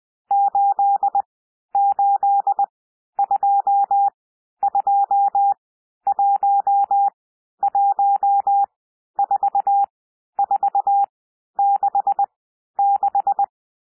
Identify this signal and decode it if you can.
Morse code, heard as a keyed tone, about 20 wpm, spaced as for 11 wpm, 800 Hz. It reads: 8822114466
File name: cwmp3 / N1 SOUND NUMBERS 20 wpm